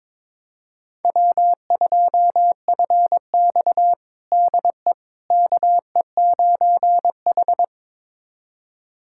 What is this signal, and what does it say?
Morse code, heard as a keyed tone, about 22 wpm, 700 Hz. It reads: W2FX DE KE9H
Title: USA Callsign DE Callsign - 4 Characters - 22wpm